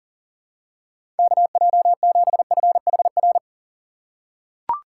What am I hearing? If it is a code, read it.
Morse code, heard as a keyed tone, about 37 wpm, 700 Hz. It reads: KJ7FHR
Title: USA Call Signs - 6 Characters - 3x Character Spacing - 40wpm